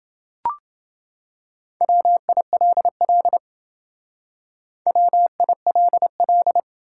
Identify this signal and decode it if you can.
Morse code, heard as a keyed tone, about 28 wpm, 700 Hz. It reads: WILL WILL